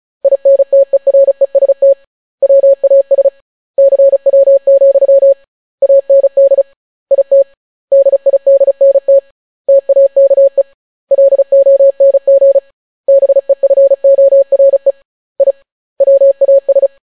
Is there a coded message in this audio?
Yes: Morse code from 0.2 to 16.9 s, heard as a keyed tone, about 36 wpm, 550 Hz. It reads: INTEREST WAS CW, AND IT DIDNT TAKE LONG BEFORE I WAS